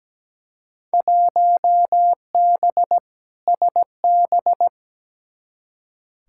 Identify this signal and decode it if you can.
Morse code, heard as a keyed tone, about 17 wpm, 700 Hz. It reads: 1B SB
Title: Field Day Exchanges - 17wpm